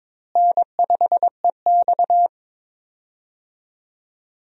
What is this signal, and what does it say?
Morse code, heard as a keyed tone, about 22 wpm, 700 Hz. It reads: N5EX
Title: USA Call Signs - 4 Characters - Repeated 2x - 22wpm